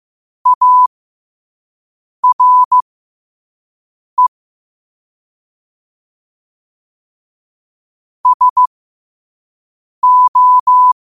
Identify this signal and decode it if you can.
Morse code, heard as a keyed tone, about 15 wpm, spaced as for 5 wpm, 1000 Hz. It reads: ARE SO
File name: qotdfiles / 5wpm